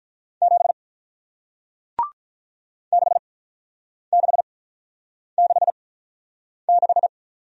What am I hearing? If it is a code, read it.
Morse code, heard as a keyed tone, about 47 wpm, spaced as for 9 wpm, 700 Hz. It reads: 7 6666